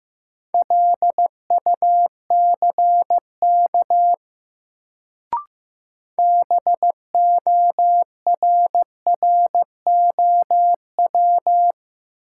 Morse code, heard as a keyed tone, about 15 wpm, 700 Hz. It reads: LUCK BORROW
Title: Sentences from Top 1500 Words - New Word Drill - 15wpm